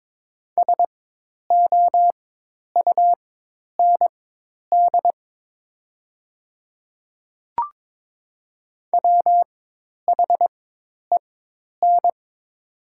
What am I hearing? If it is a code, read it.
Morse code, heard as a keyed tone, about 22 wpm, spaced as for 10 wpm, 700 Hz. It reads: SOUND WHEN
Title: 4x Character Spacing - Top 100 Words - 22wpm